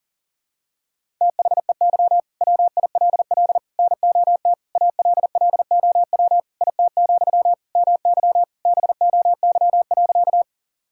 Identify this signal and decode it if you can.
Morse code, heard as a keyed tone, about 40 wpm, 700 Hz. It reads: THEY WILL NOT ALLOW IT, MY BOY.
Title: Sentences from Top 1000 Words - Review Part 5 - 40wpm